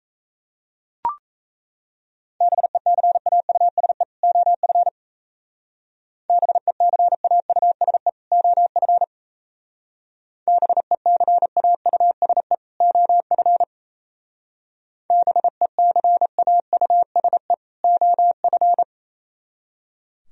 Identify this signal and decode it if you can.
Morse code, heard as a keyed tone, about 35 wpm, 700 Hz. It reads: BECAUSE OF BECAUSE OF BECAUSE OF BECAUSE OF